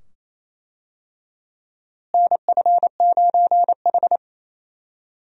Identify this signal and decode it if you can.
Morse code, heard as a keyed tone, about 28 wpm, 700 Hz. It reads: NF9H
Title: USA Call Signs - 4 Characters - Repeated 2x - 28wpm